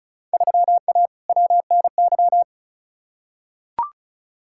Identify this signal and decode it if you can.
Morse code, heard as a keyed tone, about 34 wpm, 700 Hz. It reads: 3A WNY
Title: Field Day Exchanges - 35wpm